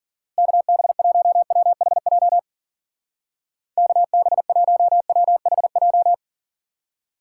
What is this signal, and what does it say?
Morse code, heard as a keyed tone, about 49 wpm, 700 Hz. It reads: KB1WHJ KB1WHJ
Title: Speed Racing - USA 2x3 Call Signs 40wpm